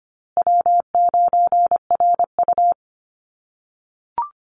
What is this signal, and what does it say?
Morse code, heard as a keyed tone, about 25 wpm, 700 Hz. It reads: W9RU
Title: USA Call Signs Repeated 2x 25wpm